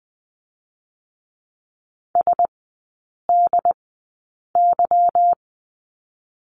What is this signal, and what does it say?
Morse code, heard as a keyed tone, about 20 wpm, spaced as for 8 wpm, 700 Hz. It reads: SDY
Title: 184 Lesson 30e - Letter b - Three Characters 20wpm